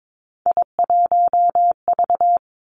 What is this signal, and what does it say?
Morse code, heard as a keyed tone, about 22 wpm, 700 Hz. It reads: I1V
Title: Three Challenging Letter-Number Combinations 22wpm